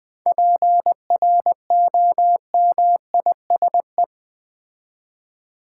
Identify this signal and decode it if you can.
Morse code, heard as a keyed tone, about 20 wpm, 700 Hz. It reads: PROMISE
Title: 7 Letter Words - First 3 Letters Given - 20wpm